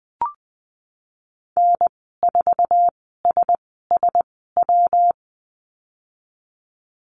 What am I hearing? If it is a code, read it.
Morse code, heard as a keyed tone, about 20 wpm, spaced as for 15 wpm, 700 Hz. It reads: N4SSW
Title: 122 Lesson 20g - Number 3 - Call Signs 20wpm